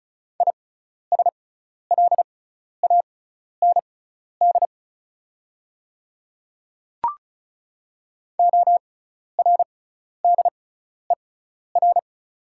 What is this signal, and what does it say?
Morse code, heard as a keyed tone, about 34 wpm, spaced as for 12 wpm, 690 Hz. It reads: ISLAND ORDER
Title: 6x Character Spacing - Top 500 Words - 35wpm